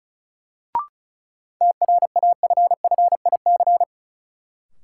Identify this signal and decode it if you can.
Morse code, heard as a keyed tone, about 36 wpm, 700 Hz. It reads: TRAFFIC
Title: Sentences from Top 1300 Words - New Word Drill - 35wpm